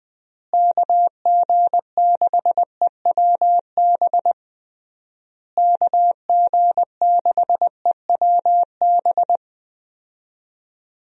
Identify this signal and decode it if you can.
Morse code, heard as a keyed tone, about 21 wpm, 700 Hz. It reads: KG6EWB KG6EWB
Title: USA Call Signs - 6 Characters - Repeated 3x - 20wpm